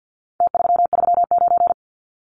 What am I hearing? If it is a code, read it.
Morse code, heard as a keyed tone, about 49 wpm, 700 Hz. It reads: T339
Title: Numbers - Contest Style - 4-Digit 50wpm